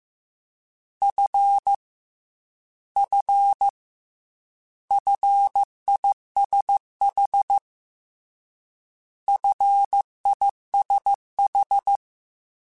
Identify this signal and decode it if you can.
Morse code, heard as a keyed tone, about 15 wpm, 800 Hz. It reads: F F FISH FISH
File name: Combinations / Combinations-mp3s / Combinations@13